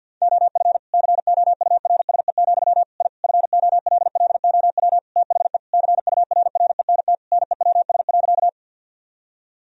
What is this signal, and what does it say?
Morse code, heard as a keyed tone, about 50 wpm, 700 Hz. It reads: OF COURSE, I FOLLOW THE CURRENT NEWS.
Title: Sentences from Top 1000 Words - Review Part 4 - 50wpm